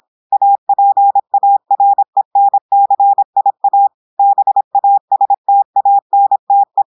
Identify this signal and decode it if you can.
Morse code, heard as a keyed tone, about 26 wpm, 800 Hz. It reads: APARENCIA BASTANTE
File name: audios / portugais / portugais texte en clair 1 vitesse 26 mots minutes